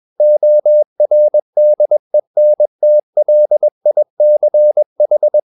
Morse code, heard as a keyed tone, about 21 wpm, 600 Hz. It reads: ORDENTLICH